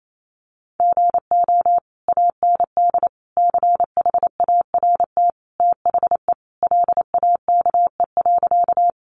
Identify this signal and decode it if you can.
Morse code, heard as a keyed tone, about 28 wpm, 700 Hz. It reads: GO AND CHART THE LAKE.
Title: Sentences from Top 1300 Words 28wpm